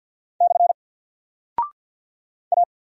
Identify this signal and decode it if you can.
Morse code, heard as a keyed tone, about 48 wpm, 700 Hz. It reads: / A
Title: Single Character (Letter:Number:Punctuation:Prosign) 50wpm